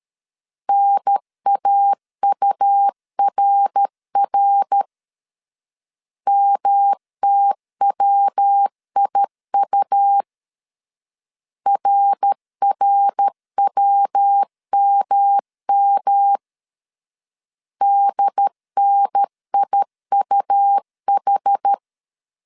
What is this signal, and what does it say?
Morse code, heard as a keyed tone, about 13 wpm, 800 Hz. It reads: NAURR MTWIU RRWMM DNIUH
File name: CW / DET / 1 3 1500C S B 12wpm 2828